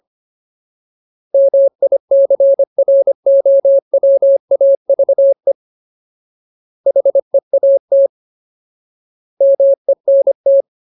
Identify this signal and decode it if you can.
Morse code, heard as a keyed tone, about 25 wpm, 550 Hz. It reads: MICROWAVE HEAT MENT
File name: Ultimate Word List / 25WPM Ultimate